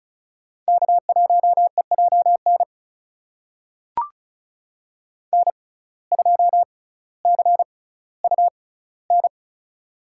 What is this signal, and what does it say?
Morse code, heard as a keyed tone, about 36 wpm, spaced as for 18 wpm, 700 Hz. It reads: K1EJN N2CUN
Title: USA Call Signs - 5 Characters - 6x Character Spacing - 35wpm